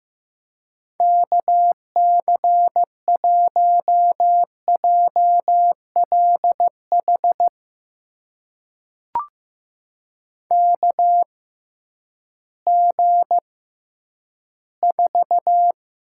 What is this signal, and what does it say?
Morse code, heard as a keyed tone, about 15 wpm, spaced as for 10 wpm, 700 Hz. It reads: KC1JLH KG4
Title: USA Call Signs - 6 Characters - 6x Character Spacing - 15wpm